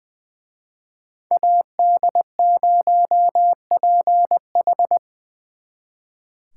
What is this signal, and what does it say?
Morse code, heard as a keyed tone, about 24 wpm, 700 Hz. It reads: AD0PH
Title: Speed Racing - USA 2x2 Call Signs 20wpm